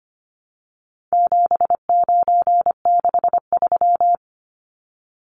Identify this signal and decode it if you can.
Morse code, heard as a keyed tone, about 25 wpm, 700 Hz. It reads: 7963
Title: Numbers - 4-Digit 25wpm